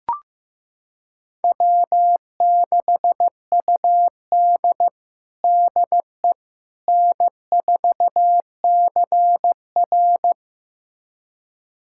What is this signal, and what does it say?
Morse code, heard as a keyed tone, about 15 wpm, 700 Hz. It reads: W6UD DE N4CR
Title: USA Callsign DE Callsign - 4 Characters - 15wpm